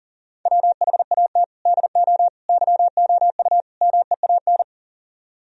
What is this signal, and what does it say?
Morse code, heard as a keyed tone, about 50 wpm, 700 Hz. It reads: WHAT DO YOU MEAN
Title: Speed-Racing - Sets of 4 Words 40wpm